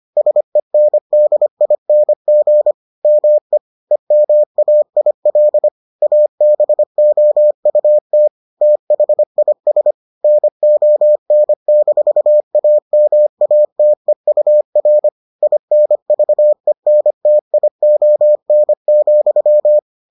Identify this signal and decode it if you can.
Morse code, heard as a keyed tone, about 25 wpm, 600 Hz. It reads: SENDING ME EMAIL ABOUT THIS NON-AMATEUR INVENTION,